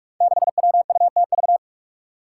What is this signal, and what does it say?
Morse code, heard as a keyed tone, about 42 wpm, 700 Hz. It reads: 6WUTV
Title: Five Challenging Letter-Number Combinations 45wpm